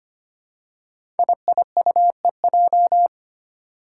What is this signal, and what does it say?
Morse code, heard as a keyed tone, about 25 wpm, 700 Hz. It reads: IIUEJ
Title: Five Challenging Letter-Number Combinations 25wpm